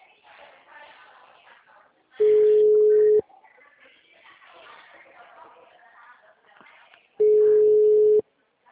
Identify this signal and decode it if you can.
Ringback tone (Europe, 425 Hz), with speech.